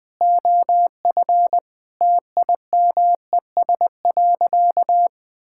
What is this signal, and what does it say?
Morse code, heard as a keyed tone, about 20 wpm, 700 Hz. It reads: OF TIMES.
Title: Sentences with 8-Word Phrases - 20wpm